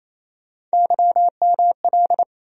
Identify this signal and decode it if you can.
Morse code, heard as a keyed tone, about 28 wpm, 700 Hz. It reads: YML
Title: USA Call Sign Postfixes - 3 Letters 28wpm